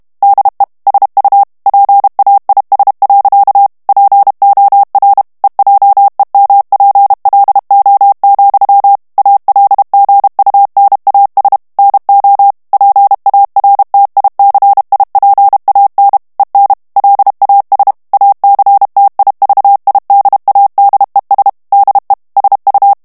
Morse code, heard as a keyed tone, about 32 wpm, 800 Hz. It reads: DE SU PAIS. POR EJEMPLO, ALGUNAS NO PARTICIPAN EN LAS ACTIVIDADES DE SU
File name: morse mp3 / ESPAGNOL TEXTE EN CLAIR 1 VITESSE 30 MOTS MINUTES